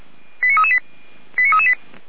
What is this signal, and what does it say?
Five-tone selective call (ZVEI), heard as quick runs of five tones, tones of about 70 ms. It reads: 89338 89338
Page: /455/SDR/ZVEI/